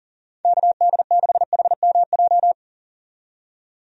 Morse code, heard as a keyed tone, about 40 wpm, 700 Hz. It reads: KD6HMJ
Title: USA Call Signs - 6 Characters - 40wpm